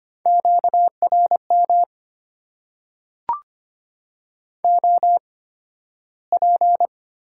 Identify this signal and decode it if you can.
Morse code, heard as a keyed tone, about 25 wpm, spaced as for 13 wpm, 700 Hz. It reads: QRM OP
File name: Top 90 QSO Elements - 8x Character Spacing - 25wpm